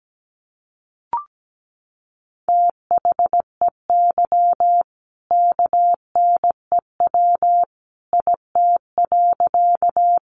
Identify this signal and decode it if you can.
Morse code, heard as a keyed tone, about 17 wpm, 700 Hz. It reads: THEY KNEW IT.
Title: Sentences from Top 400 Words 17wpm